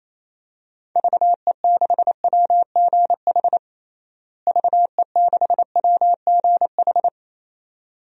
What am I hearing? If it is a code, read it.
Morse code, heard as a keyed tone, about 28 wpm, 700 Hz. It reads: VE6WGH VE6WGH
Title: DXCC-100 Call Signs - 6 Characters - Repeated 2x - 28wpm